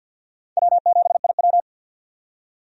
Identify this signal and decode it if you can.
Morse code, heard as a keyed tone, about 48 wpm, 700 Hz. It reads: W7IW
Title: USA Call Signs - 4 Characters - 50wpm